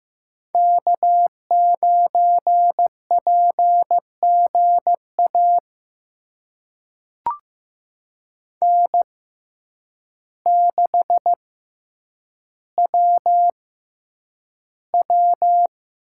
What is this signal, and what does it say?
Morse code, heard as a keyed tone, about 15 wpm, spaced as for 8 wpm, 700 Hz. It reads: K9PGA N6WW
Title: USA Call Signs - 5 Characters - 6x Character Spacing - 15wpm